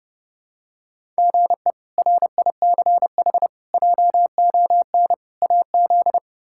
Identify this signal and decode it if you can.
Morse code, heard as a keyed tone, about 30 wpm, 700 Hz. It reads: GE RICH JON AZ
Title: K1USN SST QSO Practice - 30wpm